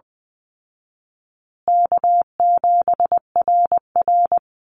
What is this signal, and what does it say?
Morse code, heard as a keyed tone, about 19 wpm, 700 Hz. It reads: K7RR
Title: Call Signs Repeated 3x - 20wpm